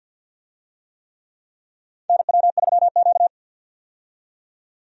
Morse code, heard as a keyed tone, about 52 wpm, 700 Hz. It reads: NW3Q
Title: USA Call Signs - 4 Characters - 50wpm